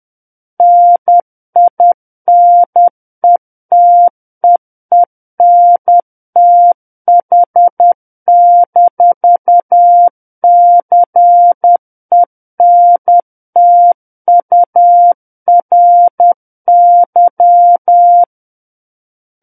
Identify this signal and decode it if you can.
Morse code, heard as a keyed tone, about 10 wpm, 700 Hz. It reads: NINETEENTH-CENTURY
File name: Book One / 01-Chapter 1 - 10 WPM